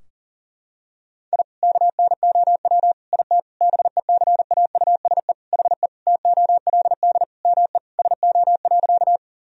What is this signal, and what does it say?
Morse code, heard as a keyed tone, about 40 wpm, 700 Hz. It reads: I KNOW IT BECAUSE HE TOLD ME SO.
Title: Sentences from Top 1000 Words - Review Part 3 - 40wpm